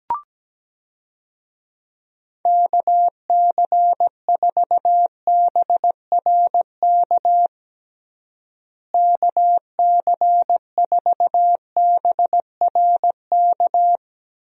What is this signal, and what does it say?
Morse code, heard as a keyed tone, about 17 wpm, 700 Hz. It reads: KC4BRK KC4BRK
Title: USA Call Signs - 6 Characters - Repeated 2x - 17wpm